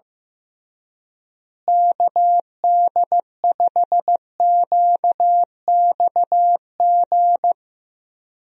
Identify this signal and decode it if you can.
Morse code, heard as a keyed tone, about 15 wpm, 700 Hz. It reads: KD5QXG
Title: USA Call Signs - 6 Characters - Repeated 3x - 15wpm